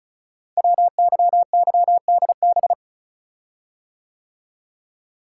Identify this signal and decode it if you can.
Morse code, heard as a keyed tone, about 36 wpm, 700 Hz. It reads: WYYDB
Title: Code Groups - 5 Letters - 35wpm